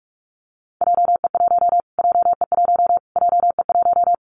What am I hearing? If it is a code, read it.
Morse code, heard as a keyed tone, about 44 wpm, 700 Hz. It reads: JE1 JE1 JE1
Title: Callsign Prefixes for Top 25 DXCC - Repeated 3x - 45wpm